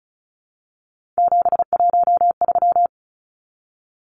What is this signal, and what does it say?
Morse code, heard as a keyed tone, about 34 wpm, 700 Hz. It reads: 713
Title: Numbers - 3-Digit 35wpm